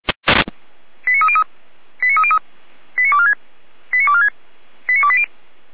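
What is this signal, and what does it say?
Five-tone selective call (ZVEI), heard as quick runs of five tones, tones of about 70 ms. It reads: 89303 89303 89246 89246 89280
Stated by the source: ZVEI five-tone selective call 89168 received by SDR.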